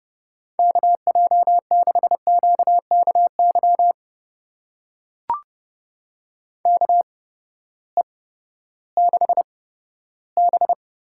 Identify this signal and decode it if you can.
Morse code, heard as a keyed tone, about 30 wpm, spaced as for 15 wpm, 700 Hz. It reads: KJ6QKY KE6B
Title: USA Call Signs - 6 Characters - 8x Character Spacing - 30wpm